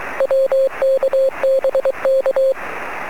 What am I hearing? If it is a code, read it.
Morse code, heard as a keyed tone, about 23 wpm, 540 Hz. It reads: WKBK
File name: sm7lcb / 06-yl3ag 23cm